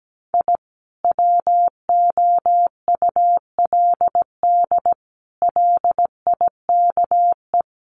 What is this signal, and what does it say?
Morse code, heard as a keyed tone, about 17 wpm, 700 Hz. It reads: I WOULD LIKE
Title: Sets of 4 Words 17wpm